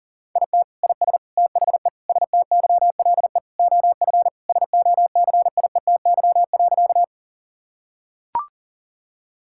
Morse code, heard as a keyed tone, about 40 wpm, 700 Hz. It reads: IT IS THE STYLE OF SOCIETY.
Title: Sentences from Top 1000 Words - Review Part 5 - 40wpm